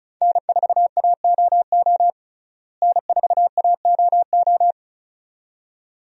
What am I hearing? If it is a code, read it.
Morse code, heard as a keyed tone, about 36 wpm, 700 Hz. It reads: N4AOO N4AOO
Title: USA Call Signs - 5 Characters - Repeated 2x - 35wpm